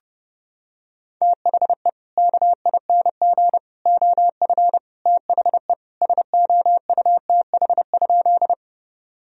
Morse code, heard as a keyed tone, about 30 wpm, 700 Hz. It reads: THE KING OF THE SOUTH?
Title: Sentences from Top 1000 Words - Review Part 2 - 30wpm